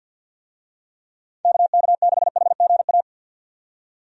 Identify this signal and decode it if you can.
Morse code, heard as a keyed tone, about 51 wpm, 700 Hz. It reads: KK6HGA
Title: USA Call Signs - 6 Characters - 50wpm